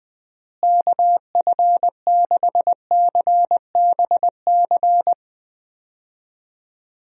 Morse code, heard as a keyed tone, about 20 wpm, 700 Hz. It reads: KF6CBC
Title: USA Call Signs - 6 Characters - Repeated 2x - 20wpm